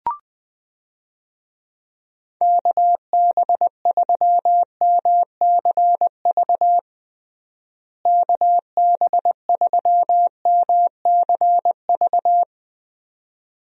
Morse code, heard as a keyed tone, about 20 wpm, 700 Hz. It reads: KB3MCV KB3MCV
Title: USA Call Signs - 6 Characters - Repeated 2x - 20wpm